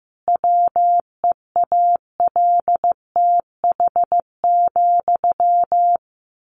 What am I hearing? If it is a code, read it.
Morse code, heard as a keyed tone, about 15 wpm, 700 Hz. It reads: WEALTH,
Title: Sentences from Top 1200 Words 15wpm